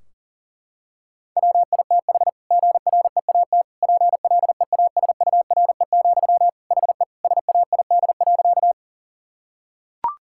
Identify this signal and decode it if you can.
Morse code, heard as a keyed tone, about 40 wpm, 700 Hz. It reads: WITH GREAT PLEASURE, HE SAID.
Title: Sentences from Top 1600 Words 40wpm